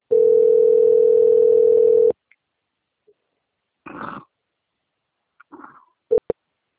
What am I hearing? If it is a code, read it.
Ringback tone (North America), 440 + 480 Hz.